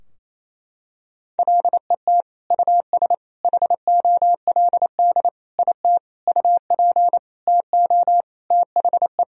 Morse code, heard as a keyed tone, about 28 wpm, 700 Hz. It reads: LET US HOLD IT UP TO THE
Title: Sentences from Top 1000 Words - Review Part 2 - 28wpm